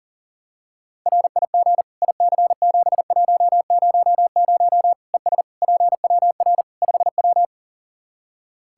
Morse code, heard as a keyed tone, about 40 wpm, 700 Hz. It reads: RIG IC7100 ES PWR 5W
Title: Practice QSO - Exchange 2 - Info - 40wpm